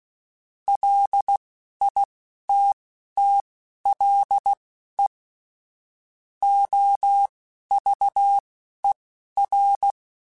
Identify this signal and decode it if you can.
Morse code, heard as a keyed tone, about 16 wpm, spaced as for 11 wpm, 800 Hz. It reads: LITTLE OVER
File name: QSOs / QSOs-mp3s / kc7sac@10